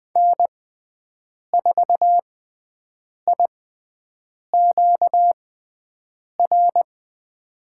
Morse code, heard as a keyed tone, about 20 wpm, spaced as for 7 wpm, 700 Hz. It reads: N4IQR
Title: USA Call Signs - 5 Characters - 6x Character Spacing - 20wpm